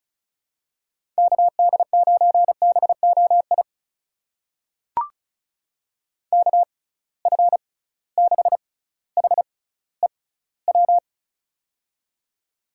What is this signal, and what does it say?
Morse code, heard as a keyed tone, about 36 wpm, spaced as for 18 wpm, 700 Hz. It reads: KD9BOI KF6HEW
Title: USA Call Signs - 6 Characters - 6x Character Spacing - 35wpm